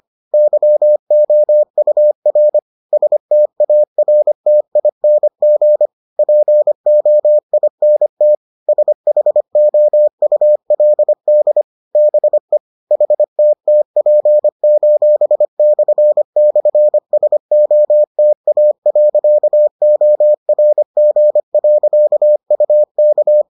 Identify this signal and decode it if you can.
Morse code, heard as a keyed tone, about 25 wpm, 600 Hz. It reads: YOUR STARTING POINT SHOULD BE HTTP://SOTA.ORG.UK